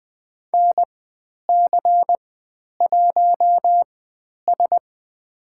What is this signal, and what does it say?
Morse code, heard as a keyed tone, about 20 wpm, spaced as for 10 wpm, 700 Hz. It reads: NC1S
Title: 095 Lesson 16g - Letter c - Call Signs 20wpm